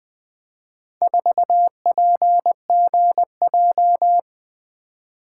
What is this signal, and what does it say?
Morse code, heard as a keyed tone, about 24 wpm, 700 Hz. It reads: 4PGJ
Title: Speed-Racing - Four Letter-Number Combinations - 20wpm